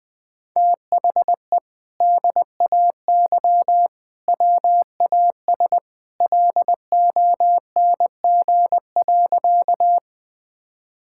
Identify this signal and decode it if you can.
Morse code, heard as a keyed tone, about 20 wpm, 700 Hz. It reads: THE DAY WAS LONG.
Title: Sentences with 2-Word Phrases - Spoken Beginning - 20wpm